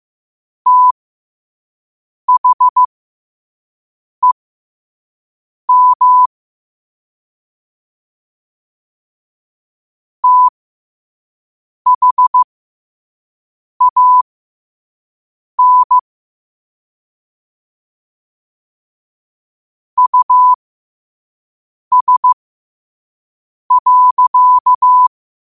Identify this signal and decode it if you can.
Morse code, heard as a keyed tone, about 15 wpm, spaced as for 5 wpm, 1000 Hz. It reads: THEM THAN US.
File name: qotdfiles / 5wpm